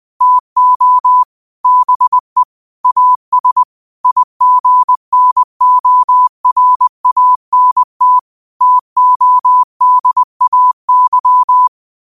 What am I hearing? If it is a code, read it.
Morse code, heard as a keyed tone, about 20 wpm, 1000 Hz. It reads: TO BE AS IGNORANT TODAY
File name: qotdfiles / 20wpm